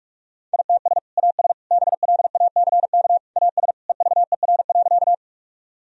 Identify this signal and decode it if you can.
Morse code, heard as a keyed tone, about 41 wpm, 700 Hz. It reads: ITS AS BLACK AS EVER.
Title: Sentences from Top 1000 Words - Review Part 2 - 45wpm